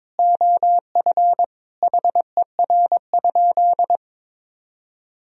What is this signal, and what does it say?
Morse code, heard as a keyed tone, about 22 wpm, 700 Hz. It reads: OF HER?
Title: Sentences from Top 1000 Words - Review Part 2 - 22wpm